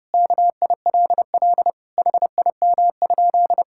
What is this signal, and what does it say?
Morse code, heard as a keyed tone, about 30 wpm, 700 Hz. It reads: KILL HIM?
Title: Sentences from Top 1600 Words 30wpm